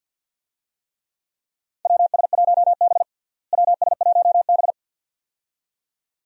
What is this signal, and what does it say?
Morse code, heard as a keyed tone, about 49 wpm, 700 Hz. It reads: WS1B WS1B
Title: USA Call Signs - 4 Characters - Repeated 2x - 50wpm